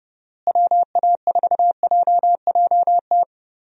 Morse code, heard as a keyed tone, about 30 wpm, 700 Hz. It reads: WA4JJT
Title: USA Call Signs - 6 Characters - Repeated 2x - 30wpm